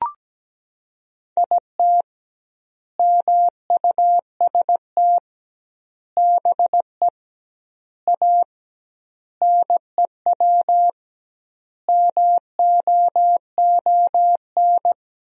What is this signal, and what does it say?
Morse code, heard as a keyed tone, about 17 wpm, 700 Hz. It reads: IT MUST BE A NEW MOON
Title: 2x Word-Spacing - Sentences from Top 500 Words - 17wpm